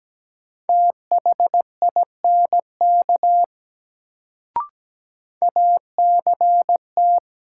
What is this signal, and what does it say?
Morse code, heard as a keyed tone, about 17 wpm, 700 Hz. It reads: THINK ACT
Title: Sentences from Top 200 Words - New Word Drill - 17wpm